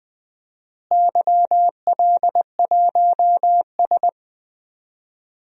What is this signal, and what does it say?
Morse code, heard as a keyed tone, about 20 wpm, 700 Hz. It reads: YL1S
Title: DXCC-100 Call Signs - 4 Characters - Repeated 2x - 20wpm